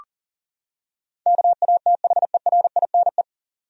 Morse code, heard as a keyed tone, about 40 wpm, 700 Hz. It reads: KATHERINE
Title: USA Common Names - 40wpm